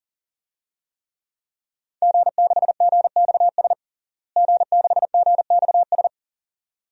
Morse code, heard as a keyed tone, about 40 wpm, 700 Hz. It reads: G6GXS G6GXS
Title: DXCC-100 Call Signs - 5 Characters - Repeated 2x - 40wpm